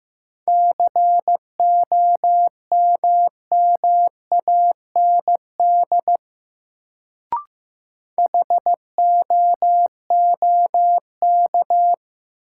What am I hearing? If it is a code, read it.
Morse code, heard as a keyed tone, about 16 wpm, 700 Hz. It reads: COMMAND HOOK
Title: Sentences from Top 1600 Words - New Word Drill Alt - 15wpm